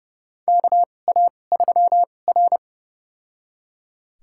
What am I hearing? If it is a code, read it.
Morse code, heard as a keyed tone, about 31 wpm, spaced as for 22 wpm, 700 Hz. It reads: KA3R
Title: USA Call Signs - 2x Character Spacing - 30wpm